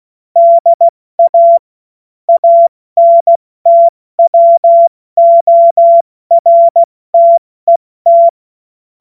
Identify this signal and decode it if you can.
Morse code, heard as a keyed tone, about 16 wpm, 680 Hz. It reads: DA ANTWORTET
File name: Morsekurs / Witze / Witz 28. September 2021, 70 F80